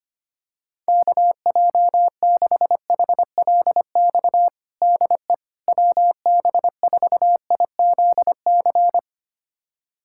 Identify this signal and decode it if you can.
Morse code, heard as a keyed tone, about 25 wpm, 700 Hz. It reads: KJ6HLX DE WB4IZC
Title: USA Callsign DE Callsign - 6 Characters - 25wpm